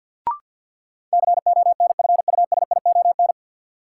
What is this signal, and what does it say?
Morse code, heard as a keyed tone, about 47 wpm, 700 Hz. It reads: CONFUSION
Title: Sentences from Top 1600 Words - New Word Drill - 50wpm